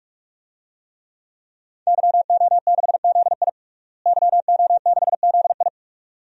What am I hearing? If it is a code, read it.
Morse code, heard as a keyed tone, about 41 wpm, 700 Hz. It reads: YO6ZI YO6ZI
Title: DXCC-100 Call Signs - 5 Characters - Repeated 2x - 45wpm